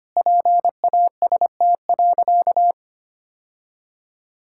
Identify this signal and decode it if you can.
Morse code, heard as a keyed tone, about 25 wpm, 700 Hz. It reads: PAST.
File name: practice / 25wpm / Sentences with 2-Word Phrases - 25wpm